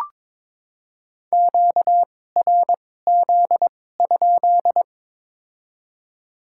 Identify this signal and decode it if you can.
Morse code, heard as a keyed tone, about 21 wpm, spaced as for 16 wpm, 700 Hz. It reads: QRZ?
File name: Top 90 QSO Elements - 2x Character Spacing - 22wpm